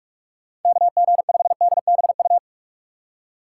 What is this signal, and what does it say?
Morse code, heard as a keyed tone, about 46 wpm, 700 Hz. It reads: KG5DBU
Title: USA Call Signs - 6 Characters - 3x Character Spacing - 45wpm